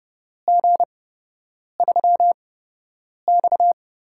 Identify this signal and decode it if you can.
Morse code, heard as a keyed tone, about 30 wpm, spaced as for 8 wpm, 700 Hz. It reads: G3X
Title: USA Call Signs - 8x Character Spacing - 30wpm